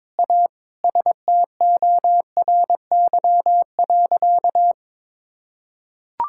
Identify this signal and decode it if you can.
Morse code, heard as a keyed tone, about 22 wpm, 700 Hz. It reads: A STORY.
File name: practice / 22wpm / Sentences with 4-Word Phrases - 22wpm